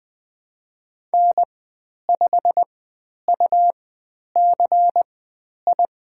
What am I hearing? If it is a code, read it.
Morse code, heard as a keyed tone, about 20 wpm, spaced as for 10 wpm, 700 Hz. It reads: N5UCI
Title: 109 Lesson 18g - Letter m - Call Signs 20wpm